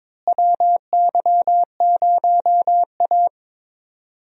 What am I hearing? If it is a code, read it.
Morse code, heard as a keyed tone, about 22 wpm, 700 Hz. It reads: WY0A
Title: USA Call Signs - 4 Characters - Repeated 2x - 22wpm